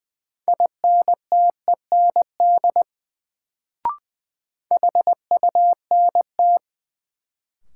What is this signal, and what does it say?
Morse code, heard as a keyed tone, about 21 wpm, 700 Hz. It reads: INTEND HUNT
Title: Sentences from Top 1600 Words - New Word Drill - 20wpm